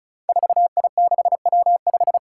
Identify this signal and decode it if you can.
Morse code, heard as a keyed tone, about 34 wpm, 700 Hz. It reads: 4I6W5
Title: Five Challenging Letter-Number Combinations 35wpm